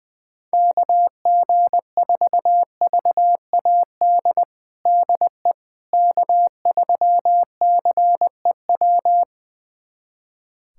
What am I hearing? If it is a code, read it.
Morse code, heard as a keyed tone, about 20 wpm, 700 Hz. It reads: KG4VAD DE K3CEW
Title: USA Callsign DE Callsign 20wpm